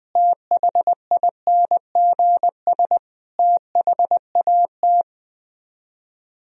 Morse code, heard as a keyed tone, about 20 wpm, 700 Hz. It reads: THINGS THAT
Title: Speed-Racing - Sets of 5 Words 20wpm